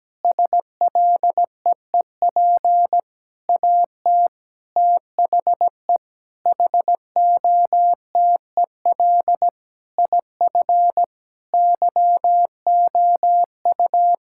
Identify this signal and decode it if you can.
Morse code, heard as a keyed tone, about 17 wpm, 700 Hz. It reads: SLEEP AT THE HOTEL IF YOU
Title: Sentences from Top 1600 Words 17wpm